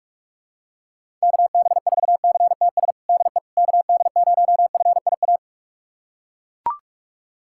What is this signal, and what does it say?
Morse code, heard as a keyed tone, about 41 wpm, 700 Hz. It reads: KB4CTS DE KD0FIA
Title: USA Callsign DE Callsign 45wpm